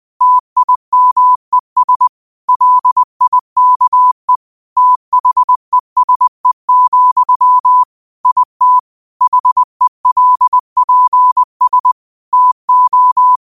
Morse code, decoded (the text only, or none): TIMES LIKE THESE, IT HELPS TO